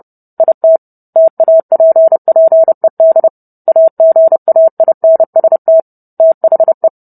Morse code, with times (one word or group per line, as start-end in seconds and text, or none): 0.4-0.8 IT
1.2-3.3 TAPPED
3.7-5.8 AGAINST
6.2-6.9 THE